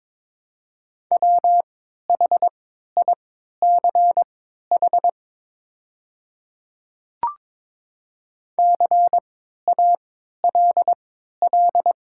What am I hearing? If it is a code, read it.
Morse code, heard as a keyed tone, about 22 wpm, spaced as for 12 wpm, 700 Hz. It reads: WHICH CALL
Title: 3x Character Spacing - Top 200 Words - 22wpm